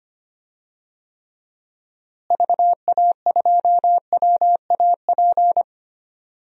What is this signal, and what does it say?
Morse code, heard as a keyed tone, about 25 wpm, 700 Hz. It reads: VA2WAP
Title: DXCC-100 Call Signs - 6 Characters - Repeated 2x - 25wpm